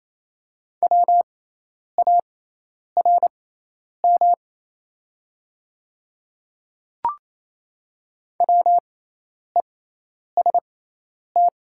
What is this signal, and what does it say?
Morse code, heard as a keyed tone, about 27 wpm, spaced as for 10 wpm, 700 Hz. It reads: WARM WEST
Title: 6x Character Spacing - Top 500 Words - 28wpm